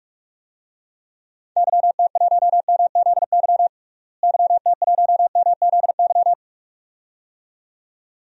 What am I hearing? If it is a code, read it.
Morse code, heard as a keyed tone, about 42 wpm, 700 Hz. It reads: YT1MZY YT1MZY
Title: DXCC-100 Call Signs - 6 Characters - Repeated 2x - 45wpm